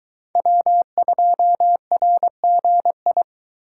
Morse code, heard as a keyed tone, about 22 wpm, 700 Hz. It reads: W2RGI